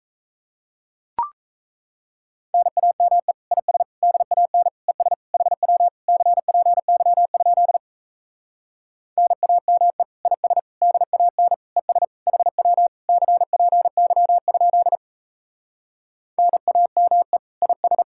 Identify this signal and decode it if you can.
Morse code, heard as a keyed tone, about 38 wpm, 700 Hz. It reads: NAME IS DAN ES HW CPY? NAME IS DAN ES HW CPY? NAME IS